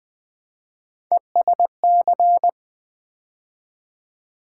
Morse code, heard as a keyed tone, about 20 wpm, 700 Hz. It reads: ESC
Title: Build A Word - 3 Letters At A Time - 8-Letter Words - 20wpm